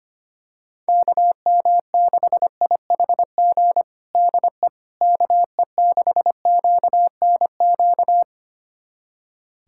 Morse code, heard as a keyed tone, about 25 wpm, 700 Hz. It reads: KM6IHG DE KE6QNQ
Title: USA Callsign DE Callsign - 6 Characters - 25wpm